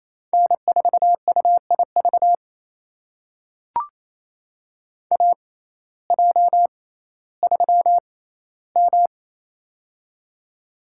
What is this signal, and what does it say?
Morse code, heard as a keyed tone, about 28 wpm, spaced as for 15 wpm, 700 Hz. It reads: N4UIV AJ3M
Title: USA Call Signs - 6x Character Spacing - 28wpm